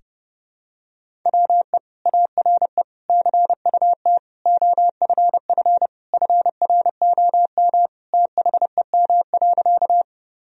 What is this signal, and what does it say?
Morse code, heard as a keyed tone, about 30 wpm, 700 Hz. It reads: WE ARE CUT OFF FROM THEM.
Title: Sentences from Top 1000 Words - Review Part 1 - 30wpm